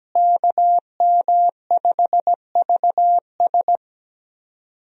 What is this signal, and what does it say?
Morse code, heard as a keyed tone, about 17 wpm, 700 Hz. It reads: KM5VS
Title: USA Call Signs - 5 Characters - Repeated 3x - 17wpm